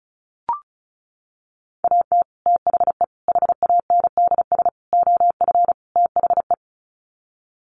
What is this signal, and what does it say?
Morse code, heard as a keyed tone, about 34 wpm, 700 Hz. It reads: AT THE HANDS OF THE
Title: Sets of 5 Words Encore2 35wpm